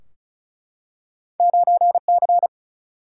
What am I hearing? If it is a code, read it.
Morse code, heard as a keyed tone, about 36 wpm, 700 Hz. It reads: 9C
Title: Single-Letter Single-Number 35wpm